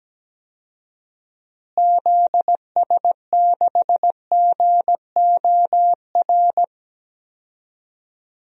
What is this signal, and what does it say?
Morse code, heard as a keyed tone, about 17 wpm, 700 Hz. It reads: ZS6GOR
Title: DXCC-100 Call Signs - 6 Characters - Repeated 3x - 17wpm